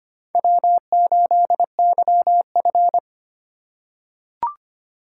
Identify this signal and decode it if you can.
Morse code, heard as a keyed tone, about 25 wpm, 700 Hz. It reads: W8YF
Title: USA Call Signs - 2x Character Spacing - 25wpm